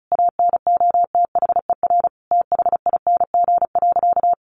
Morse code, heard as a keyed tone, about 34 wpm, 700 Hz. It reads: ANOTHER THING.
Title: Sentences from Top 700 Words 35wpm